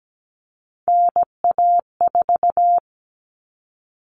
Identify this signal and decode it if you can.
Morse code, heard as a keyed tone, about 17 wpm, 700 Hz. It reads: NA4
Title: All Cut Numbers - 3-Digit 17wpm